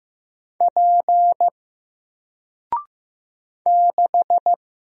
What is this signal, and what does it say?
Morse code, heard as a keyed tone, about 15 wpm, 700 Hz. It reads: P 6
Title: Single Character (Letter:Number:Punctuation:Prosign) Alt 15wpm